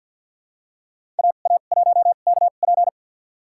Morse code, heard as a keyed tone, about 49 wpm, spaced as for 36 wpm, 700 Hz. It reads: AA1KP
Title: USA Call Signs - 5 Characters - 2x Character Spacing - 50wpm